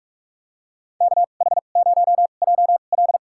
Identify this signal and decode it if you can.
Morse code, heard as a keyed tone, about 47 wpm, spaced as for 33 wpm, 700 Hz. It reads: KH0JL
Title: USA Call Signs - 5 Characters - 2x Character Spacing - 45wpm